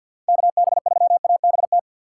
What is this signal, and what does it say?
Morse code, heard as a keyed tone, about 52 wpm, 700 Hz. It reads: KB3ABT
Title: USA Call Signs - 6 Characters - 3x Character Spacing - 50wpm